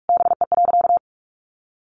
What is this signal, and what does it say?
Morse code, heard as a keyed tone, about 44 wpm, 700 Hz. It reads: BE.
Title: A Sound Of Thunder 45wpm